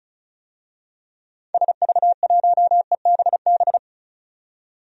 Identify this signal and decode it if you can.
Morse code, heard as a keyed tone, about 36 wpm, 700 Hz. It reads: SV1EBB
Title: DXCC-100 Call Signs - 6 Characters - 35wpm